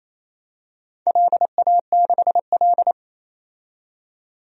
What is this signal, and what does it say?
Morse code, heard as a keyed tone, about 28 wpm, 700 Hz. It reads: LA6L
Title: DXCC-100 Call Signs - 4 Characters - 28wpm